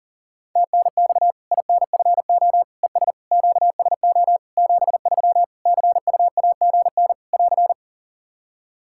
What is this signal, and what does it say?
Morse code, heard as a keyed tone, about 40 wpm, 700 Hz. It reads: TNX INFO ES QSO 73 CUAGN <AR>
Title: Practice QSO - Exchange 4 - Ending - 40wpm